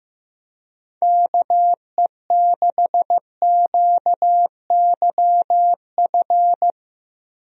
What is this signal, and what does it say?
Morse code, heard as a keyed tone, about 15 wpm, 700 Hz. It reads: KE6QYF
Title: USA Call Signs - 6 Characters - Repeated 3x - 15wpm